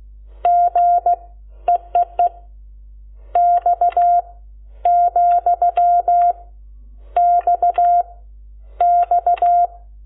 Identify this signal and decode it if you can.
Morse code, heard as a keyed tone, about 16 wpm, 700 Hz. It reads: G EEE X , X X
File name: Lecciones / CLASE Nº12